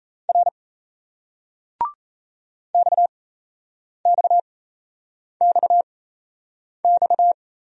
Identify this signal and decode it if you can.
Morse code, heard as a keyed tone, about 38 wpm, spaced as for 8 wpm, 700 Hz. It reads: R XXXX